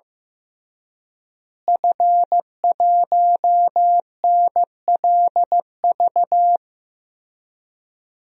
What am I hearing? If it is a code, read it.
Morse code, heard as a keyed tone, about 15 wpm, 700 Hz. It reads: F1NLV
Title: DXCC-100 Call Signs - 5 Characters - Repeated 3x - 15wpm